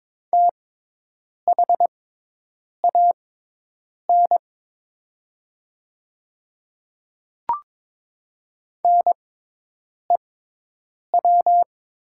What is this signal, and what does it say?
Morse code, heard as a keyed tone, about 22 wpm, spaced as for 8 wpm, 700 Hz. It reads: THAN NEW
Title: 6x Character Spacing - Top 200 Words - 22wpm